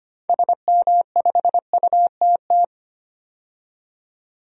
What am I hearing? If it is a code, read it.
Morse code, heard as a keyed tone, about 25 wpm, 700 Hz. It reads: SM5UTT
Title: DXCC-100 Call Signs - 6 Characters - 25wpm